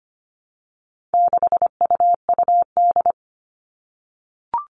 Morse code, heard as a keyed tone, about 25 wpm, 700 Hz. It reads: 6UUD
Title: All Cut Numbers - 4-Digit 25wpm